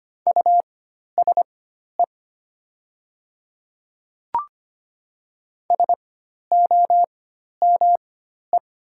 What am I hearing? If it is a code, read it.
Morse code, heard as a keyed tone, about 25 wpm, spaced as for 12 wpm, 690 Hz. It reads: USE SOME
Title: 4x Character Spacing - Top 200 Words - 25wpm